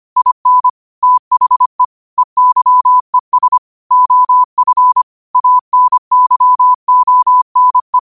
Morse code, heard as a keyed tone, about 25 wpm, 1000 Hz. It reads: IN THE EYES OF ANYONE
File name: qotdfiles / 25wpm